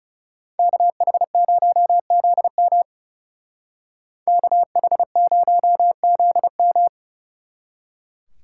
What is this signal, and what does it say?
Morse code, heard as a keyed tone, about 38 wpm, 700 Hz. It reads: KH0ZM KH0ZM